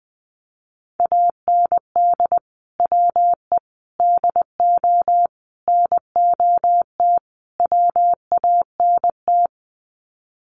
Morse code, heard as a keyed tone, about 20 wpm, 700 Hz. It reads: AND WE DO NOT WANT
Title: Sets of 5 Words Encore2 20wpm